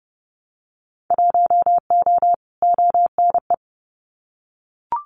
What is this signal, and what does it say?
Morse code, heard as a keyed tone, about 30 wpm, 700 Hz. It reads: 1O ONE
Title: Winter Field Day Exchanges 30wpm